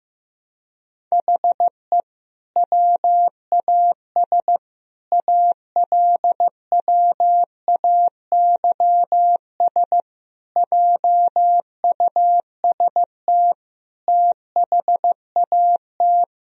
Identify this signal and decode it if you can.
Morse code, heard as a keyed tone, about 15 wpm, 700 Hz. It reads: HE WAS ALWAYS JUST THAT
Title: Sentences from Top 1000 Words - Review Part 4 - 15wpm